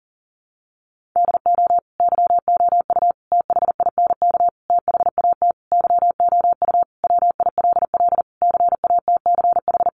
Morse code, heard as a keyed tone, about 40 wpm, 700 Hz. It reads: DO YOU THINK THAT YOU WILL CATCH
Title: Sentences from Top 1500 Words 40wpm